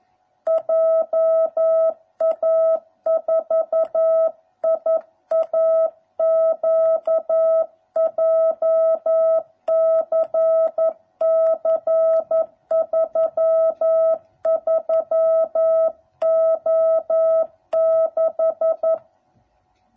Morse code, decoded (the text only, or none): JA4IAQ JCC 33O6